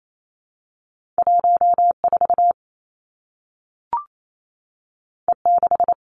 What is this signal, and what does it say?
Morse code, heard as a keyed tone, about 28 wpm, 700 Hz. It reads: 1 4 E 6
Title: Two Challenging Letter-Number Combinations 28wpm